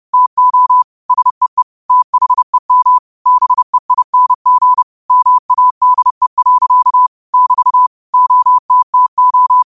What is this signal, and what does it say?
Morse code, heard as a keyed tone, about 30 wpm, 1000 Hz. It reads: TO SEE THEM BEING MADE. <BT> OTTO
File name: qotdfiles / 30wpm